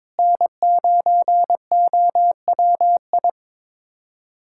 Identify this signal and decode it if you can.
Morse code, heard as a keyed tone, about 21 wpm, 700 Hz. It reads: N9OWI
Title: USA Callsign DE Callsign 22wpm